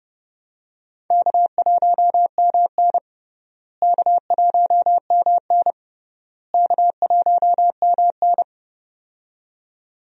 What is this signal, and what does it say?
Morse code, heard as a keyed tone, about 30 wpm, 700 Hz. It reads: K1MN K1MN K1MN
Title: USA Call Signs - 4 Characters - Repeated 3x - 30wpm